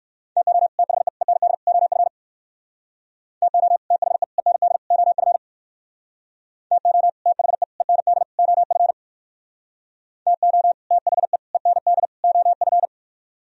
Morse code, heard as a keyed tone, about 58 wpm, 700 Hz. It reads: TO THE END OF TO THE END OF TO THE END OF TO THE END OF